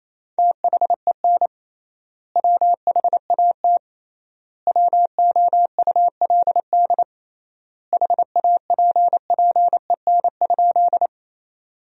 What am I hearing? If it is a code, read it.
Morse code, heard as a keyed tone, about 28 wpm, 700 Hz. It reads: THEN WHAT WOULD HAPPEN?
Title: 3x Word-Spacing - Sentences from Top 400 Words - 28wpm